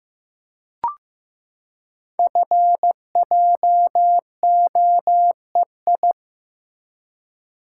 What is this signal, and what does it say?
Morse code, heard as a keyed tone, about 15 wpm, 700 Hz. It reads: FJOEI
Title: Code Groups - 5 Letters - 15wpm